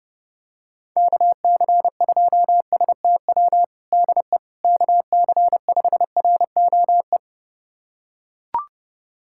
Morse code, heard as a keyed tone, about 30 wpm, 700 Hz. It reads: KC2STW DE KC5ROE
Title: USA Callsign DE Callsign - 6 Characters - 30wpm